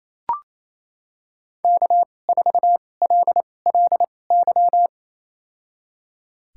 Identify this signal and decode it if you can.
Morse code, heard as a keyed tone, about 28 wpm, spaced as for 20 wpm, 700 Hz. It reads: K4LLY
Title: USA Call Signs - 5 Characters - 2x Character Spacing - 28wpm